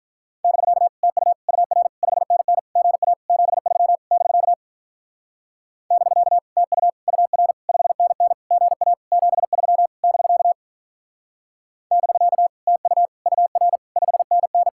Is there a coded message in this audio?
Yes: Morse code from 0.4 to 14.7 s, heard as a keyed tone, about 44 wpm, 700 Hz. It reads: <BK> TU UR 5NN GA 73 <BK> <BK> TU UR 5NN GA 73 <BK> <BK> TU UR 5NN